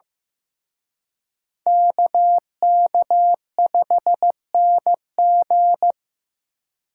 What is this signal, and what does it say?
Morse code, heard as a keyed tone, about 18 wpm, 700 Hz. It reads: KK5NG
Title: Speed Racing - USA 2x2 Call Signs 15wpm